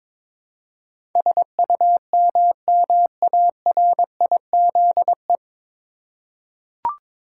Morse code, heard as a keyed tone, about 22 wpm, 700 Hz. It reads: SUMMARIZE
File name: 9 Letter Words - First 3 Letters Given - 22wpm